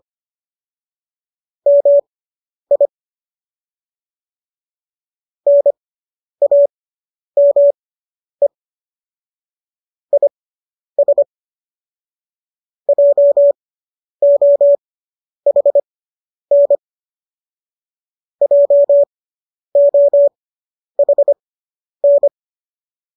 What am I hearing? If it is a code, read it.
Morse code, heard as a keyed tone, about 25 wpm, spaced as for 10 wpm, 570 Hz. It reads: MI NAME IS JOHN JOHN